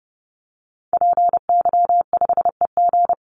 Morse code, heard as a keyed tone, about 30 wpm, 700 Hz. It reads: PY5EG
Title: Call Signs - 30wpm